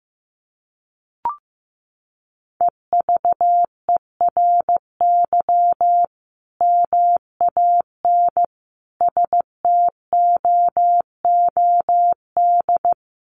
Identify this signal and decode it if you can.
Morse code, heard as a keyed tone, about 15 wpm, 700 Hz. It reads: EVERY MAN STOOD
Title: Sentences from Top 1400 Words 15wpm